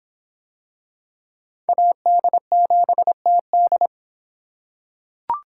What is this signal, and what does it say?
Morse code, heard as a keyed tone, about 26 wpm, 700 Hz. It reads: AD7TD